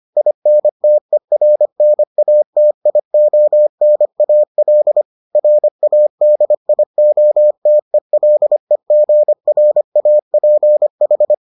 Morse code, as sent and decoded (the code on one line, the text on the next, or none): .. -. - . .-. -. .- - .. --- -. .- .-.. / .-. .- -.. .. --- - . .-.. --. .-. .- .--. ....
INTERNATIONAL RADIOTELEGRAPH